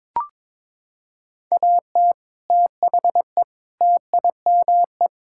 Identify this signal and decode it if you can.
Morse code, heard as a keyed tone, about 22 wpm, 700 Hz. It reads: AT THE TIME
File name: practice / 22wpm / Sets of 6 Words - 22wpm